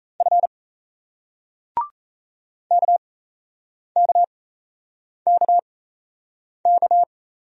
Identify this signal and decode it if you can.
Morse code, heard as a keyed tone, about 39 wpm, spaced as for 8 wpm, 700 Hz. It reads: F KKKK